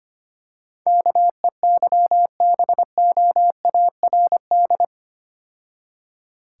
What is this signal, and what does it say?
Morse code, heard as a keyed tone, about 28 wpm, 700 Hz. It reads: KEYBOARD
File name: practice / 25wpm / Difficult Words - Speed-Racing with Hints - 25wpm